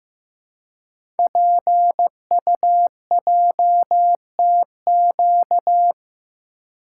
Morse code, heard as a keyed tone, about 15 wpm, 700 Hz. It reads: PUJTQ
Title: Code Groups - 5 Letter:Numbers - 15wpm